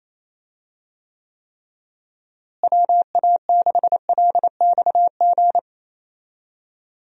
Morse code, heard as a keyed tone, about 28 wpm, 700 Hz. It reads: WA6LXG
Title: USA Call Signs - 6 Characters - Repeated 3x - 28wpm